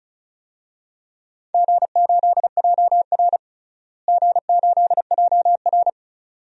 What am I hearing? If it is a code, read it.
Morse code, heard as a keyed tone, about 36 wpm, 700 Hz. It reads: G8JR G8JR
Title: DXCC-100 Call Signs - 4 Characters - Repeated 2x - 35wpm